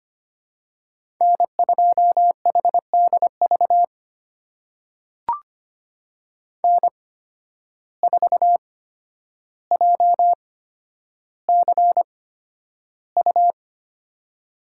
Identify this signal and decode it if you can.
Morse code, heard as a keyed tone, about 25 wpm, spaced as for 11 wpm, 700 Hz. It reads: N2HDV N4JCU
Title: USA Call Signs - 5 Characters - 8x Character Spacing - 25wpm